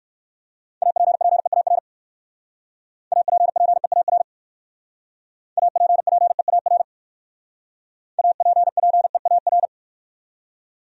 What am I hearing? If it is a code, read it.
Morse code, heard as a keyed tone, about 68 wpm, 700 Hz. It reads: APPEAR APPEAR APPEAR APPEAR